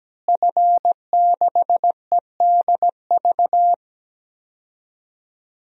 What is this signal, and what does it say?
Morse code, heard as a keyed tone, about 17 wpm, 700 Hz. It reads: F6EDV
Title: DXCC-100 Call Signs - 5 Characters - Repeated 2x - 17wpm